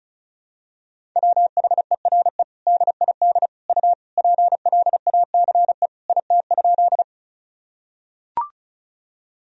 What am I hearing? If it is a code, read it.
Morse code, heard as a keyed tone, about 36 wpm, 690 Hz. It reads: WHERE DID U PLACE IT?
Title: Sentences from Top 200 Words and Abbreviations 35wpm